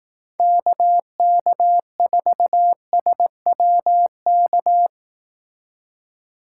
Morse code, heard as a keyed tone, about 18 wpm, 700 Hz. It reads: KK4SWK
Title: Speed Racing - USA 2x3 Call Signs 15wpm